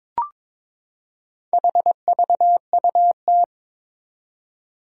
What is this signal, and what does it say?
Morse code, heard as a keyed tone, about 22 wpm, 700 Hz. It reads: HVUT
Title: Four Challenging Letter-Number Combinations 22wpm